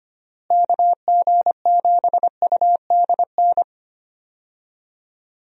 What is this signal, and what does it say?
Morse code, heard as a keyed tone, about 25 wpm, 700 Hz. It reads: KG7UDN
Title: USA Call Signs - 6 Characters - Repeated 3x - 25wpm